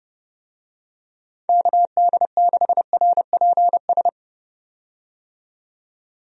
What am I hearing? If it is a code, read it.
Morse code, heard as a keyed tone, about 30 wpm, 700 Hz. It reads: KD6RPS
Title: USA Call Signs - 6 Characters - 30wpm